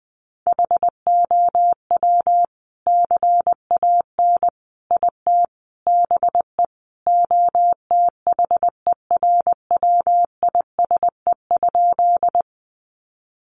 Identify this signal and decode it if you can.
Morse code, heard as a keyed tone, about 20 wpm, 700 Hz. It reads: HOW CAN IT BE OTHERWISE?
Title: Sentences from Top 1200 Words 20wpm